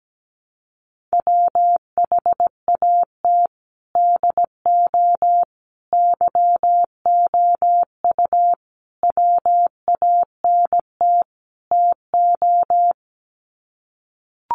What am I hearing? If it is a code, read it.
Morse code, heard as a keyed tone, about 17 wpm, 700 Hz. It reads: WHAT DO YOU WANT TO
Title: Sets of 5 Words 17wpm